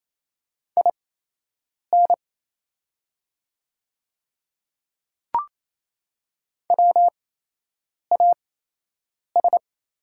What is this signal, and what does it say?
Morse code, heard as a keyed tone, about 28 wpm, spaced as for 8 wpm, 700 Hz. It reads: IN WAS
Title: 8x Character Spacing - Top 100 Words - 28wpm